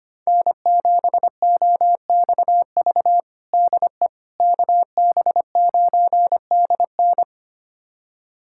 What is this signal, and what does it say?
Morse code, heard as a keyed tone, about 25 wpm, 700 Hz. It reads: N7OXV DE KB9DN
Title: USA Callsign DE Callsign - 5 Characters - 25wpm